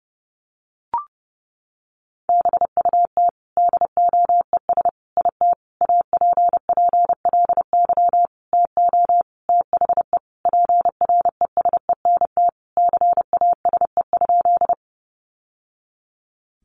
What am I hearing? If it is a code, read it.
Morse code, heard as a keyed tone, about 30 wpm, 700 Hz. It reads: BUT DOES IT APPLY TO THE PRESENT CASE?
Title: Sentences from Top 1200 Words 30wpm